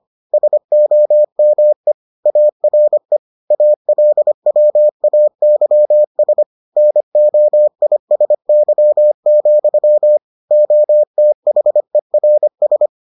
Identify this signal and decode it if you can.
Morse code, heard as a keyed tone, about 25 wpm, 600 Hz. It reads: SOME ARE ALWAYS NOISY, OTHERS